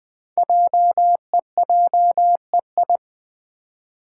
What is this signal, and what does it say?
Morse code, heard as a keyed tone, about 20 wpm, 700 Hz. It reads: JEJEI
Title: Code Groups - 5 Letter:Numbers - 20wpm